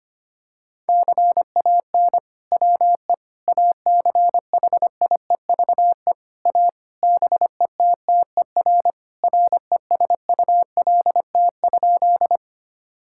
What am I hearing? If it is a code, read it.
Morse code, heard as a keyed tone, about 25 wpm, 700 Hz. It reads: CAN WE ACHIEVE A BETTER RESULT?
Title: Sentences from Top 1000 Words - Review Part 6 - 25wpm